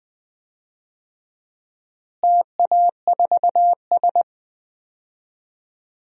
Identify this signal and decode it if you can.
Morse code, heard as a keyed tone, about 20 wpm, 700 Hz. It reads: TA4S
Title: DXCC-100 Call Signs - 4 Characters - Repeated 3x - 20wpm